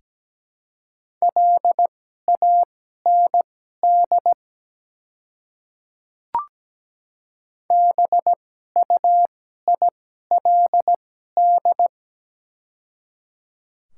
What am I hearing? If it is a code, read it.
Morse code, heard as a keyed tone, about 17 wpm, spaced as for 12 wpm, 700 Hz. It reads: LAND BUILD
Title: 2x Character Spacing - Top 300 Words - 17wpm